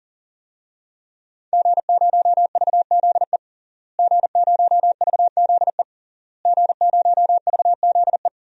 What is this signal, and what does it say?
Morse code, heard as a keyed tone, about 40 wpm, 700 Hz. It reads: G0VZE G0VZE G0VZE
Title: DXCC-100 Call Signs - 5 Characters - Repeated 3x - 40wpm